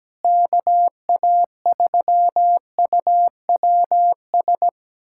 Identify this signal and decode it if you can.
Morse code, heard as a keyed tone, about 17 wpm, 700 Hz. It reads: KA3UWS
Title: USA Callsign DE Callsign 17wpm